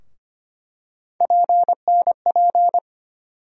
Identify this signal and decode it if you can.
Morse code, heard as a keyed tone, about 25 wpm, 700 Hz. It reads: PNP
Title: USA Call Sign Postfixes - 3 Letters 25wpm